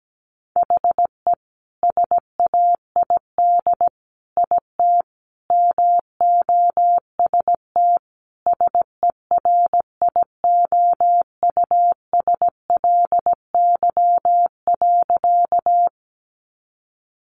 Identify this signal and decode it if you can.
Morse code, heard as a keyed tone, about 17 wpm, 700 Hz. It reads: HE SAID IT MOST SERIOUSLY.
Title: Sentences from Top 1200 Words 17wpm